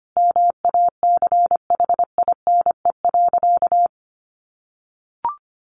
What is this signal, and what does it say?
Morse code, heard as a keyed tone, about 25 wpm, 700 Hz. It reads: MACHINE.
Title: A Sound Of Thunder 25wpm